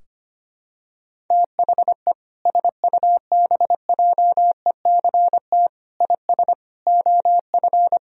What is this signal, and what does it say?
Morse code, heard as a keyed tone, about 25 wpm, 700 Hz. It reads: THE SUBJECT IS OF
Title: Sentences from Top 1000 Words - Review Part 4 - 25wpm